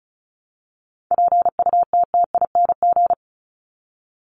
Morse code, heard as a keyed tone, about 34 wpm, 700 Hz. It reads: PUTTING
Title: ING Suffix - Top 100 Words with Base Words 35wpm